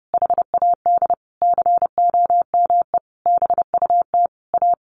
Morse code, heard as a keyed tone, about 30 wpm, 700 Hz. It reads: HAD COME BUT A
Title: Sentences from Top 1400 Words 30wpm